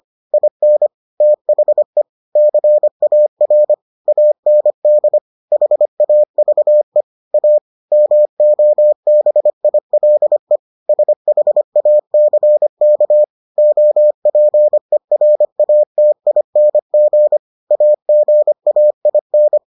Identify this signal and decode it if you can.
Morse code, heard as a keyed tone, about 25 wpm, 600 Hz. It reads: IN THE CAR AND HAVE A MOBILE SHACK OPERATING AGAIN